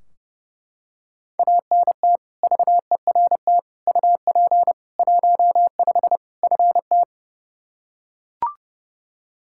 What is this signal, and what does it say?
Morse code, heard as a keyed tone, about 30 wpm, 700 Hz. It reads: ANT VERT UP 15 FT
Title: Practice QSO - Exchange 2 - Info - 30wpm